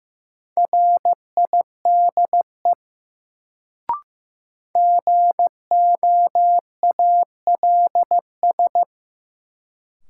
Morse code, heard as a keyed tone, about 15 wpm, 700 Hz. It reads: RIDE GOALS
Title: Sentences from Top 1000 Words - Review Part 6 - New Word Drill - 15wpm